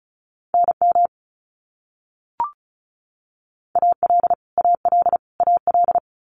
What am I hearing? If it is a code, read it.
Morse code, heard as a keyed tone, about 34 wpm, 700 Hz. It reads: NM AL AL AL
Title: US States Repeated 3x 35wpm